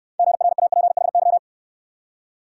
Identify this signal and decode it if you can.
Morse code, heard as a keyed tone, about 65 wpm, 700 Hz. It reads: GRAPHY